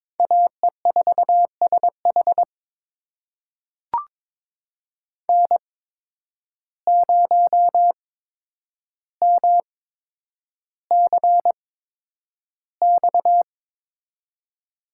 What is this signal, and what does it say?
Morse code, heard as a keyed tone, about 22 wpm, spaced as for 9 wpm, 700 Hz. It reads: AE4SH N0MCX
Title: USA Call Signs - 5 Characters - 8x Character Spacing - 22wpm